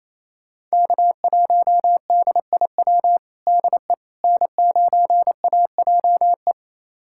Morse code, heard as a keyed tone, about 28 wpm, 700 Hz. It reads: K1DIW DE N9AJE
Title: USA Callsign DE Callsign - 5 Characters - 28wpm